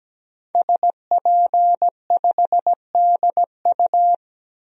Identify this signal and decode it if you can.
Morse code, heard as a keyed tone, about 17 wpm, 700 Hz. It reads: SP5DU
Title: DXCC-100 Call Signs - 5 Characters - 17wpm